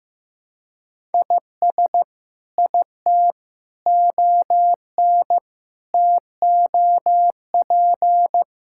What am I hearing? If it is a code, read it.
Morse code, heard as a keyed tone, about 15 wpm, 700 Hz. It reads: IS IT ON TOP
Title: Sentences from Top 1000 Words - Review Part 6 - 15wpm